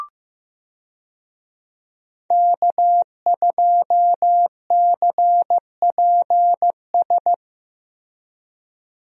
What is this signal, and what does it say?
Morse code, heard as a keyed tone, about 15 wpm, 700 Hz. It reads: K2CPS
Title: USA Call Signs - 5 Characters - Repeated 3x - 15wpm